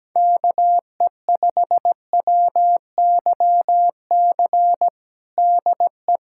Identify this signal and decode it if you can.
Morse code, heard as a keyed tone, about 17 wpm, 700 Hz. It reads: KE5WYC DE
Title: USA Callsign DE Callsign - 6 Characters - 17wpm